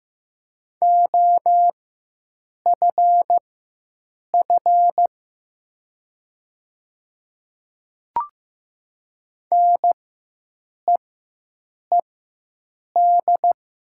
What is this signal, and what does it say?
Morse code, heard as a keyed tone, about 15 wpm, spaced as for 7 wpm, 700 Hz. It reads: OFF NEED
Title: 4x Character Spacing - Top 500 Words - 15wpm